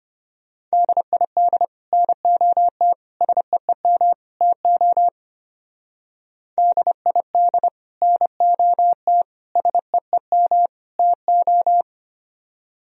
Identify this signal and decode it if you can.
Morse code, heard as a keyed tone, about 32 wpm, 700 Hz. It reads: DID NOT SEEM TO DID NOT SEEM TO